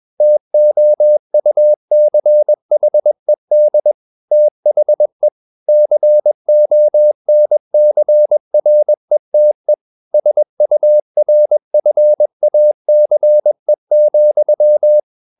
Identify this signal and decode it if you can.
Morse code, heard as a keyed tone, about 21 wpm, 600 Hz. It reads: TOUCHED THE CONCRETE SURFACE,
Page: Romanen Never af Ken Follet på engelsk.